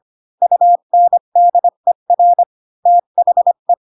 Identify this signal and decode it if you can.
Morse code, heard as a keyed tone, about 25 wpm, 700 Hz. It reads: UNDER THE
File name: Pinocchio / Pinocchio ch1 20 25 ik1hge